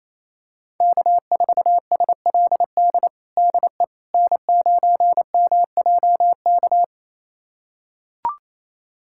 Morse code, heard as a keyed tone, about 27 wpm, 700 Hz. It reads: K4SLD DE N9MJK
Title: USA Callsign DE Callsign - 5 Characters - 28wpm